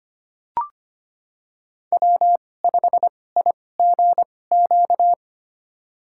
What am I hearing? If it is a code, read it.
Morse code, heard as a keyed tone, about 25 wpm, spaced as for 18 wpm, 700 Hz. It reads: W5IGQ
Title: USA Call Signs - 2x Character Spacing - 25wpm